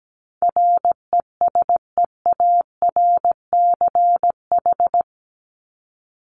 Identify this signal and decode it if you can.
Morse code, heard as a keyed tone, about 17 wpm, 700 Hz. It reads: RESEARCH
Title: Top 501-750 Words 17wpm